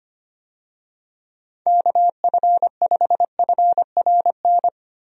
Morse code, heard as a keyed tone, about 25 wpm, 700 Hz. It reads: KF5FRN
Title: USA Call Signs - 6 Characters - Repeated 2x - 25wpm